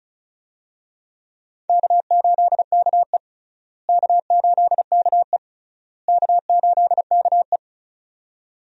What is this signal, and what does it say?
Morse code, heard as a keyed tone, about 36 wpm, 700 Hz. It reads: K8KE K8KE K8KE
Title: USA Call Signs - 4 Characters - Repeated 3x - 35wpm